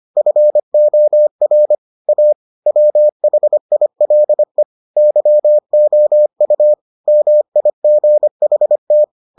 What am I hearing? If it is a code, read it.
Morse code, heard as a keyed tone, about 25 wpm, 600 Hz. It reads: FOR A WHILE YOU MIGHT